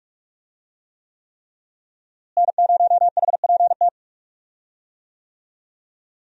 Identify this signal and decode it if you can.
Morse code, heard as a keyed tone, about 42 wpm, 700 Hz. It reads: N0HPT
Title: USA Call Signs - 5 Characters - Repeated 3x - 45wpm